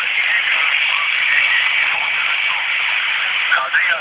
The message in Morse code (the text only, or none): none